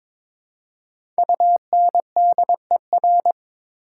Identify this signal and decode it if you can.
Morse code, heard as a keyed tone, about 21 wpm, 700 Hz. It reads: UNDER
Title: Prefixes and Suffixes 22wpm